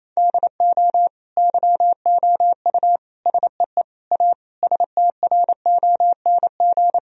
Morse code, heard as a keyed tone, about 28 wpm, 700 Hz. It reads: DO YOU SEE A STRONG
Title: Sentences from Top 1000 Words - Review Part 4 - 28wpm